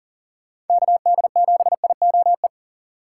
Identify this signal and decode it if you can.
Morse code, heard as a keyed tone, about 52 wpm, 700 Hz. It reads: KD7IOE